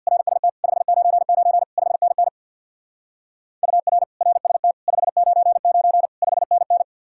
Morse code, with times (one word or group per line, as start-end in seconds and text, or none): 0.1-0.5 RST
0.6-1.6 599
1.8-2.3 5NN
3.6-4.0 UR
4.2-4.7 RST
4.9-6.1 599
6.2-6.8 5NN